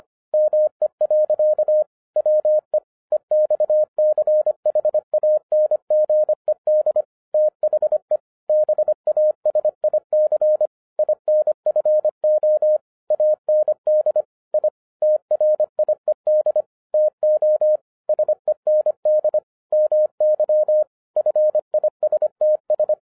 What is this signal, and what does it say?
Morse code, heard as a keyed tone, about 24 wpm, 610 Hz. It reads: ME. WE EXCHANGED THE BASIC INFO AND I TRIED TO SEND MY FISTS
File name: FIRST FISTS CONTACT-25wpm